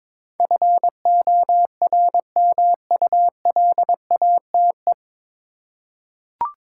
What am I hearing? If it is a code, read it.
Morse code, heard as a keyed tone, about 22 wpm, 700 Hz. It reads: FORMULATE
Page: Morse Code Ninja practice 22wpm